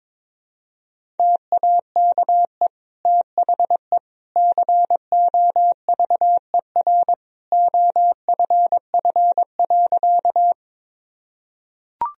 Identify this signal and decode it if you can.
Morse code, heard as a keyed tone, about 22 wpm, 700 Hz. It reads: TAKE THE COVER OFF.
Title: Sentences from Top 1000 Words - Review Part 1 - 22wpm